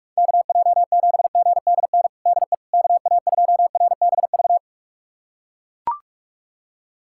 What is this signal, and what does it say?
Morse code, heard as a keyed tone, about 42 wpm, 700 Hz. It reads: KJ7GDN DE KA2RBV
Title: USA Callsign DE Callsign - 6 Characters - 45wpm